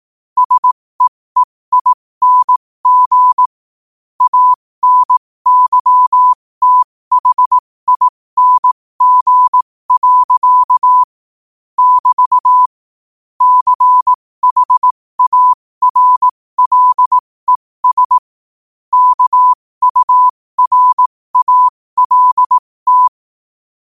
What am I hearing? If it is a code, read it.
Morse code, heard as a keyed tone, about 18 wpm, spaced as for 15 wpm, 1000 Hz. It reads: SEEING ANYTHING. <BT> CHARLES KURALT
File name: qotdfiles / 15wpm